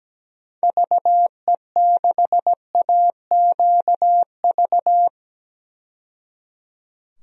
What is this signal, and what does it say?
Morse code, heard as a keyed tone, about 17 wpm, 700 Hz. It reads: VE6AQV
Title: DXCC-100 Call Signs - 6 Characters - 17wpm